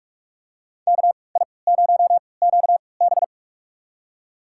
Morse code, heard as a keyed tone, about 47 wpm, spaced as for 26 wpm, 700 Hz. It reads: KI0QB
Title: USA Call Signs - 5 Characters - 3x Character Spacing - 45wpm